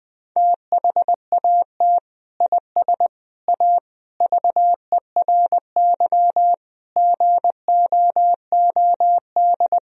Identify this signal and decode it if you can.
Morse code, heard as a keyed tone, about 20 wpm, 700 Hz. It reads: THAT IS A VERY GOOD
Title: Sentences from Top 1000 Words - Review Part 4 - 20wpm